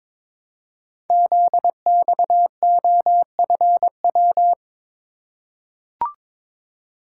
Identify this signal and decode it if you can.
Morse code, heard as a keyed tone, about 22 wpm, 700 Hz. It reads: ZXOFW
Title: Code Groups - 5 Letter:Numbers - 22wpm